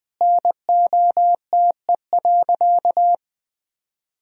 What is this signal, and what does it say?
Morse code, heard as a keyed tone, about 20 wpm, 700 Hz. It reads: NOTE.
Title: Sentences from Top 1000 Words - Review Part 2 - 20wpm